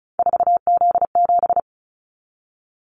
Morse code, heard as a keyed tone, about 34 wpm, 700 Hz. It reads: 4Z7
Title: Callsign Prefixes for Top 50 DXCC - 35wpm